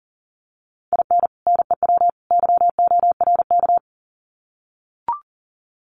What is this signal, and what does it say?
Morse code, heard as a keyed tone, about 40 wpm, 700 Hz. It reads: IN NEW YORK
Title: Sets of 3 Words 40wpm